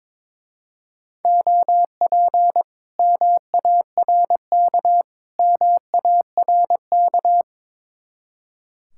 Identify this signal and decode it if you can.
Morse code, heard as a keyed tone, about 24 wpm, 700 Hz. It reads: OP MARK MARK
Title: Speed-Racing Common QSO Phrases - 22wpm